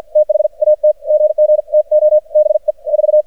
Morse code, heard as a keyed tone, about 54 wpm, spaced as for 45 wpm, 610 Hz. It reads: THAT GG TO BE 4
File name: archived / CRSnet